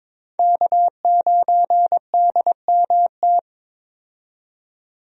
Morse code, heard as a keyed tone, about 22 wpm, 700 Hz. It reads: K9DMT
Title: USA Call Signs - 5 Characters - 22wpm